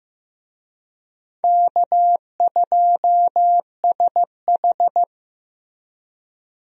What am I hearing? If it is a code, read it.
Morse code, heard as a keyed tone, about 15 wpm, 700 Hz. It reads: K2SH
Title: USA Call Signs - 4 Characters - Repeated 3x - 15wpm